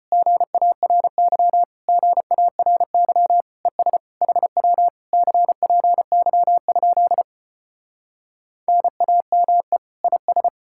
Morse code, heard as a keyed tone, about 34 wpm, 700 Hz. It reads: GARY GARY ES HW CPY? NAME IS